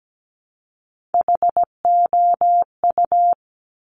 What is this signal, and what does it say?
Morse code, heard as a keyed tone, about 17 wpm, 700 Hz. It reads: HOU
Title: Top Three Letter Combinations in Words 17wpm